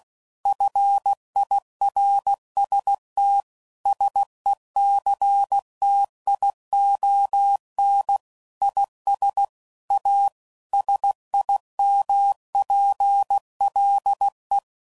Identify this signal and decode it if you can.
Morse code, heard as a keyed tone, about 16 wpm, 800 Hz. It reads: FIRST SECTION IS A SIMPLE
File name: CW Code Groups @15WPM / Technical Subjects / PwrSupply